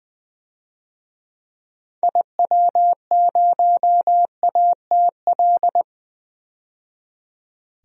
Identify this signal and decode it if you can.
Morse code, heard as a keyed tone, about 21 wpm, 700 Hz. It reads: IW0ATL
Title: DXCC-100 Call Signs - 6 Characters - 20wpm